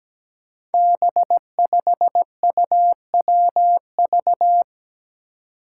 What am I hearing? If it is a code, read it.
Morse code, heard as a keyed tone, about 17 wpm, 700 Hz. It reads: B5UWV
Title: Five Challenging Letter-Number Combinations 17wpm